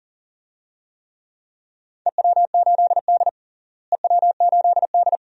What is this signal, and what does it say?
Morse code, heard as a keyed tone, about 40 wpm, 700 Hz. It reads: EW8D EW8D
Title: DXCC-100 Call Signs - 4 Characters - Repeated 2x - 40wpm